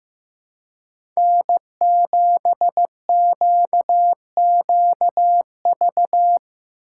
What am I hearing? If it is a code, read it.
Morse code, heard as a keyed tone, about 15 wpm, 700 Hz. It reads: N7QQV
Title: USA Call Signs - 5 Characters - 3x Character Spacing - 15wpm